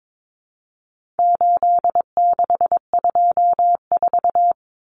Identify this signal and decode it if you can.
Morse code, heard as a keyed tone, about 22 wpm, 700 Hz. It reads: 8624
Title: Common Cut Numbers - 4-Digit 22wpm